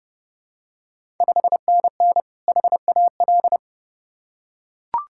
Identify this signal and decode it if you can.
Morse code, heard as a keyed tone, about 30 wpm, 700 Hz. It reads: 5NN HAL
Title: Ontario QSO Party Exchange 30wpm